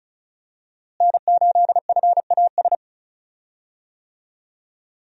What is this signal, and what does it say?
Morse code, heard as a keyed tone, about 34 wpm, 700 Hz. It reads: N8FAS
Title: USA Call Signs - 5 Characters - Repeated 2x - 35wpm